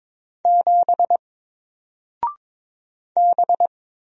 Morse code, heard as a keyed tone, about 22 wpm, 700 Hz. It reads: 7 B
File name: Single-Letter-Number-22wpm